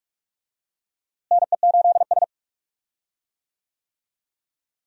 Morse code, heard as a keyed tone, about 43 wpm, 700 Hz. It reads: NE8S
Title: USA Call Signs - 4 Characters - Repeated 3x - 45wpm